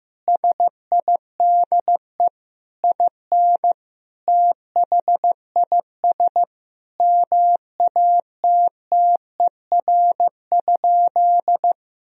Morse code, heard as a keyed tone, about 15 wpm, 700 Hz. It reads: SIDE IN THIS MATTER?
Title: Sentences from Top 1000 Words - Review Part 1 - 15wpm